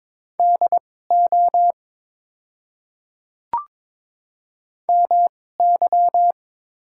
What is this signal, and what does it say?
Morse code, heard as a keyed tone, about 22 wpm, spaced as for 16 wpm, 700 Hz. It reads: DO MY
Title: 2x Character Spacing - Top 100 Words - 22wpm